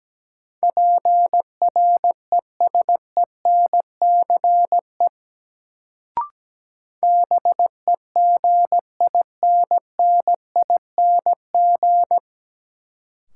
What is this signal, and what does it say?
Morse code, heard as a keyed tone, about 17 wpm, 700 Hz. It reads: PRESENCE BEGINNING
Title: Sets of 3 Words - New Word Drill - 17wpm